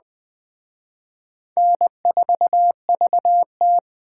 Morse code, heard as a keyed tone, about 24 wpm, 700 Hz. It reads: N4VT
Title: Speed Racing - USA 1x2 Call Signs 20wpm